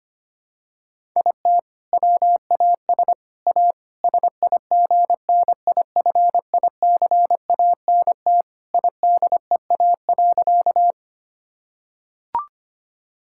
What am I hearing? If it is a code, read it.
Morse code, heard as a keyed tone, about 25 wpm, 700 Hz. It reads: IT WAS A SIGNIFICANT IDEA.
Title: Sentences from Top 1000 Words - Review Part 6 - 25wpm